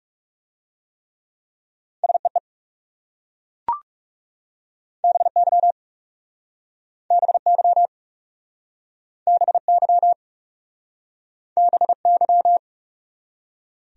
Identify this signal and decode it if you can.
Morse code, heard as a keyed tone, about 39 wpm, 700 Hz. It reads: SEE BY BY BY BY